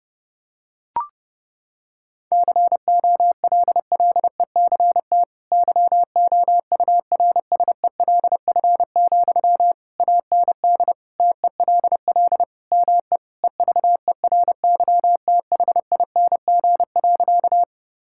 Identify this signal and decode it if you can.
Morse code, heard as a keyed tone, about 32 wpm, 700 Hz. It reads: COLLECT YOURSELF, AND TELL ME EVERYTHING.
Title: Sentences from Top 1600 Words 30wpm